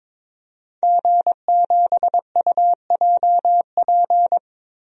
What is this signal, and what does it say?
Morse code, heard as a keyed tone, about 22 wpm, 700 Hz. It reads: G7UJP
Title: DXCC-100 Call Signs - 5 Characters - 22wpm